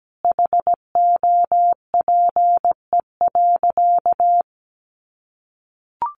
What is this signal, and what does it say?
Morse code, heard as a keyed tone, about 17 wpm, 700 Hz. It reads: HOPE.
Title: Sentences from Top 1000 Words 17wpm